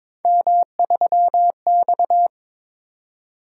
Morse code, heard as a keyed tone, about 22 wpm, 700 Hz. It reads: M3X
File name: practice / 22wpm / Three Letter-Number Combinations - 22wpm